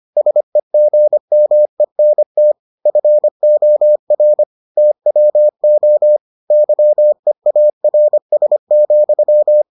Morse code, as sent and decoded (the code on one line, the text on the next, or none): ... --. -- . -. - / ..-. --- .-. / - .-- --- / -.-- . .- .-. ... --..--
SEGMENT FOR TWO YEARS,